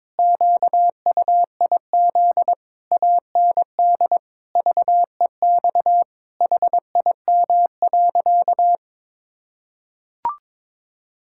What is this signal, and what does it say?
Morse code, heard as a keyed tone, about 22 wpm, 700 Hz. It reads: QUIZ AND VEX HIM.
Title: Pangram Challenge v2 - 22wpm